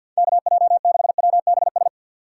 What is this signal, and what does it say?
Morse code, heard as a keyed tone, about 51 wpm, 700 Hz. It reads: KJ6WBS
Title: USA Call Signs - 6 Characters - 6x Character Spacing - 50wpm